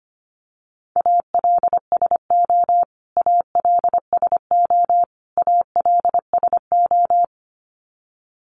Morse code, heard as a keyed tone, about 25 wpm, 700 Hz. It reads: ALSO ALSO ALSO
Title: Top 200 Words Repeated 25wpm